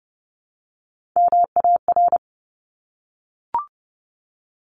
Morse code, heard as a keyed tone, about 30 wpm, 700 Hz. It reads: MAR
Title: ARRL Sections 30wpm